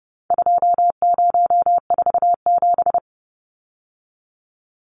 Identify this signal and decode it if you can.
Morse code, heard as a keyed tone, about 30 wpm, 700 Hz. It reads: 2047
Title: Numbers - 4-Digit 30wpm